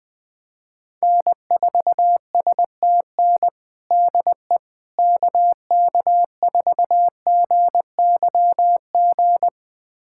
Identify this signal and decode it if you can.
Morse code, heard as a keyed tone, about 20 wpm, 700 Hz. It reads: N4STN DE KK4GYG
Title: USA Callsign DE Callsign 20wpm